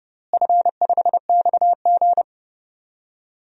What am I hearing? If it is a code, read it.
Morse code, heard as a keyed tone, about 30 wpm, 700 Hz. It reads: F5XG
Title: DXCC-100 Call Signs - 4 Characters - Repeated 2x - 30wpm